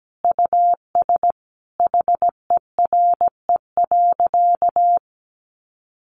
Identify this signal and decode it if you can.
Morse code, heard as a keyed tone, about 17 wpm, 700 Hz. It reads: US HERE.
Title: A Sound Of Thunder 17wpm